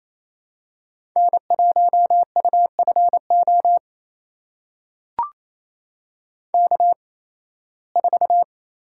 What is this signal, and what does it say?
Morse code, heard as a keyed tone, about 28 wpm, spaced as for 18 wpm, 700 Hz. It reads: N1UFO K4
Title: USA Call Signs - 5 Characters - 8x Character Spacing - 28wpm